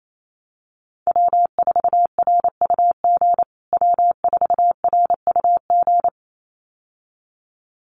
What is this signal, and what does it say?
Morse code, heard as a keyed tone, about 28 wpm, 700 Hz. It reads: W4RUG W4RUG
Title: USA Call Signs Repeated 2x 28wpm